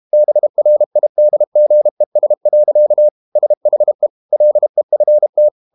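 Morse code, heard as a keyed tone, about 32 wpm, 600 Hz. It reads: BRIDGES. SHE LEFT
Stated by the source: Morse code practice at 32 wpm from CW med Gnister, reading Ken Follett's novel Never in English.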